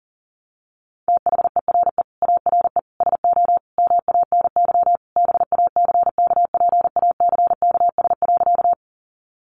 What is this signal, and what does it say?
Morse code, heard as a keyed tone, about 40 wpm, 700 Hz. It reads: THERE ARE SO MANY BACKPACKS.
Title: Additional Easy Phrases 40wpm